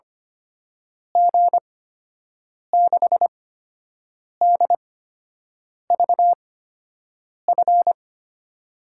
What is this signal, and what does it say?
Morse code, heard as a keyed tone, about 25 wpm, spaced as for 7 wpm, 700 Hz. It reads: G6DVF
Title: USA Call Signs - 6 Characters - 8x Character Spacing - 25wpm